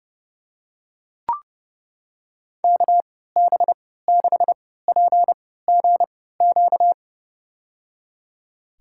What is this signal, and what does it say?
Morse code, heard as a keyed tone, about 30 wpm, spaced as for 17 wpm, 700 Hz. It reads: KB6PGQ
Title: USA Call Signs - 3x Character Spacing - 30wpm